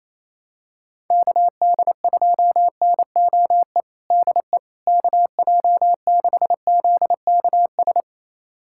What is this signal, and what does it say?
Morse code, heard as a keyed tone, about 28 wpm, 700 Hz. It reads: KD2NOE DE KJ6ZKS
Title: USA Callsign DE Callsign - 6 Characters - 28wpm